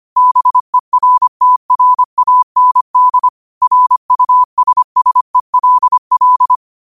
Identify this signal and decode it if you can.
Morse code, heard as a keyed tone, about 25 wpm, 1000 Hz. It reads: BERTRAND RUSSELL
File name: qotdfiles / 25wpm